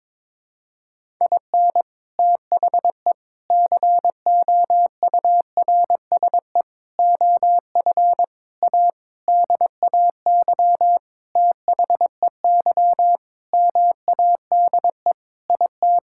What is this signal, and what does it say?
Morse code, heard as a keyed tone, about 22 wpm, 700 Hz. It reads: IN THE COURSE OF A DAY THEY MADE IT
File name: practice / 22wpm / Sentences with 4-Word Phrases - 22wpm